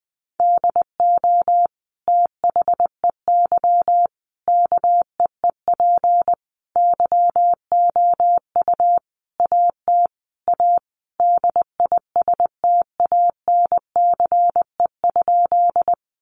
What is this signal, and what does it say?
Morse code, heard as a keyed tone, about 20 wpm, 700 Hz. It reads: DO THEY KEEP YOU AT A DISTANCE?
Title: Sentences from Top 1200 Words 20wpm